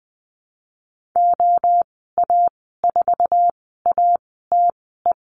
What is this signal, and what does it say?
Morse code, heard as a keyed tone, about 19 wpm, spaced as for 14 wpm, 700 Hz. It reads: OA4ATE
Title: 140 Lesson 23g - Letter f - Call Signs 20wpm